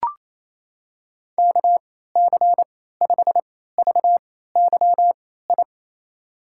Morse code, heard as a keyed tone, about 28 wpm, spaced as for 16 wpm, 700 Hz. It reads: KC5VYI
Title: USA Call Signs - 3x Character Spacing - 28wpm